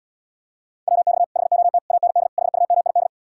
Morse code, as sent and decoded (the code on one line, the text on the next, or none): .-- -..- / .... --- - / .- -. -.. / .... ..- -- .. -..
WX HOT AND HUMID